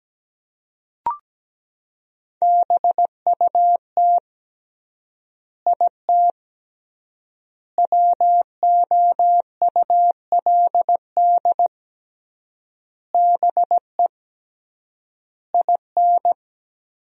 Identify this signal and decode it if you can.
Morse code, heard as a keyed tone, about 17 wpm, 700 Hz. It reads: BUT IT WOULD BE IN
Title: 3x Word-Spacing - Sentences from Top 300 Words - 17wpm